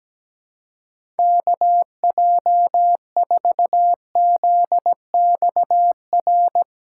Morse code, heard as a keyed tone, about 17 wpm, 700 Hz. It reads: KJ4ZXR
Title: USA Call Signs - 6 Characters - 4x Character Spacing - 17wpm